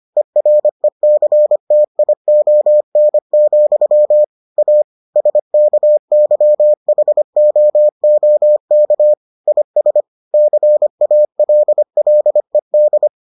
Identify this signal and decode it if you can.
Morse code, heard as a keyed tone, about 25 wpm, 600 Hz. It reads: ERECTION, A SKYHOOK IS CALLED